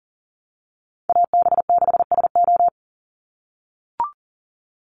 Morse code, heard as a keyed tone, about 40 wpm, 700 Hz. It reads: AB6SO
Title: NA SOTA Chaser Callsigns 40wpm